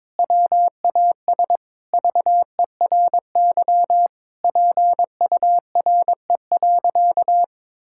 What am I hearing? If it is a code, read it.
Morse code, heard as a keyed tone, about 22 wpm, 700 Hz. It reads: WAS VERY PURE.
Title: Sentences from Top 1600 Words 22wpm